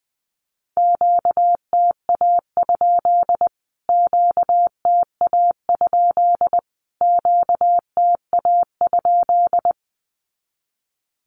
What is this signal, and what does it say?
Morse code, heard as a keyed tone, about 20 wpm, 700 Hz. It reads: QTA? QTA? QTA?
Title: Q-Codes Repeated 3x 20wpm